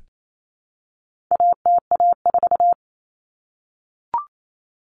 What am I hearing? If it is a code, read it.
Morse code, heard as a keyed tone, about 28 wpm, 700 Hz. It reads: ATA4
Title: Common Cut Numbers - 4-Digit 28wpm